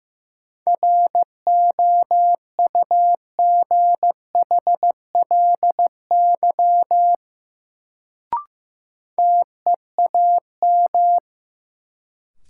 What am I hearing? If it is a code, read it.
Morse code, heard as a keyed tone, about 15 wpm, 700 Hz. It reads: ROUGHLY TEAM
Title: Sentences from Top 1500 Words - New Word Drill - 15wpm